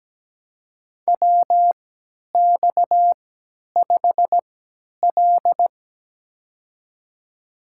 Morse code, heard as a keyed tone, about 17 wpm, spaced as for 10 wpm, 700 Hz. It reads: WX5L
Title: USA Call Signs - 3x Character Spacing - 17wpm